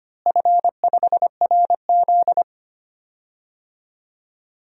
Morse code, heard as a keyed tone, about 25 wpm, 700 Hz. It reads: F5RZ
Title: DXCC-100 Call Signs - 4 Characters - Repeated 3x - 25wpm